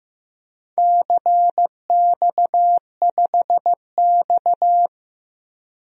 Morse code, heard as a keyed tone, about 15 wpm, 700 Hz. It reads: CX5X
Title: DXCC-100 Call Signs - 4 Characters - Repeated 2x - 15wpm